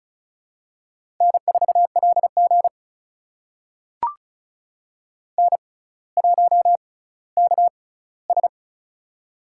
Morse code, heard as a keyed tone, about 36 wpm, spaced as for 18 wpm, 700 Hz. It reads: N4LG N1KS
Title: USA Call Signs - 4 Characters - 6x Character Spacing - 35wpm